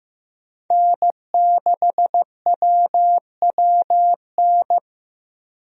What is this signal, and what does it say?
Morse code, heard as a keyed tone, about 15 wpm, 700 Hz. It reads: N6WWN
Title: USA Call Signs - 5 Characters - 15wpm